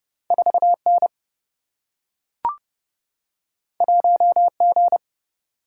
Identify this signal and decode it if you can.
Morse code, heard as a keyed tone, about 30 wpm, 700 Hz. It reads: 4 N 1 G
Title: Single-Letter Single-Number 30wpm